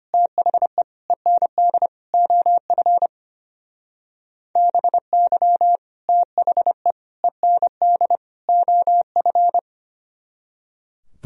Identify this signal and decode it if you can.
Morse code, heard as a keyed tone, about 32 wpm, 700 Hz. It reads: THE END OF BY THE END OF